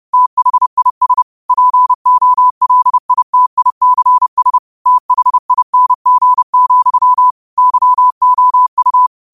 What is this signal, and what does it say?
Morse code, heard as a keyed tone, about 30 wpm, 1000 Hz. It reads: THIS POLITICS THING, YOU